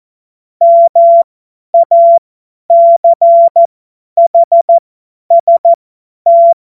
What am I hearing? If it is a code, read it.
Morse code, heard as a keyed tone, about 14 wpm, spaced as for 10 wpm, 680 Hz. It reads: MACHST
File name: Morsekurs / Witze / Witz 19. Oktober 2021, 50 F70